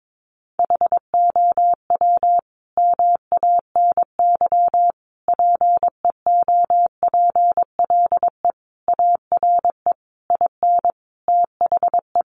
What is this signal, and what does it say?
Morse code, heard as a keyed tone, about 22 wpm, 700 Hz. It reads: HOW MANY PEOPLE ARE IN THE
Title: Sentences from Top 1200 Words 22wpm